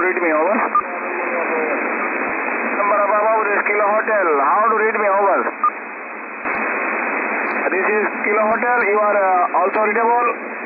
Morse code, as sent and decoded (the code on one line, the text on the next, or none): none